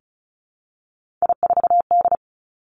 Morse code, heard as a keyed tone, about 36 wpm, 700 Hz. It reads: I4D
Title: Three Challenging Letter-Number Combinations 35wpm